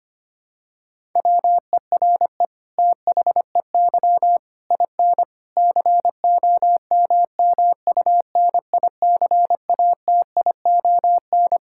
Morse code, heard as a keyed tone, about 25 wpm, 700 Hz. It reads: WERE THEY IN COMMUNICATION
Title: Sentences from Top 1000 Words - Review Part 6 - 25wpm